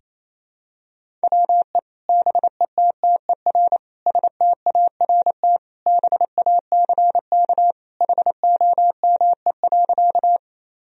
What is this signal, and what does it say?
Morse code, heard as a keyed tone, about 28 wpm, 700 Hz. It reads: WE BETTER START BACK HOME.
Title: Sentences from Top 1000 Words - Review Part 1 - 28wpm